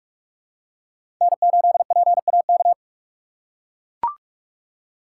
Morse code, heard as a keyed tone, about 43 wpm, 700 Hz. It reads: N8PAK
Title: USA Call Signs - 5 Characters - 2x Character Spacing - 45wpm